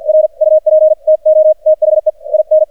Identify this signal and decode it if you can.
Morse code, heard as a keyed tone, about 46 wpm, 620 Hz. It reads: R UP TO THE IN
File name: archived / CRSnet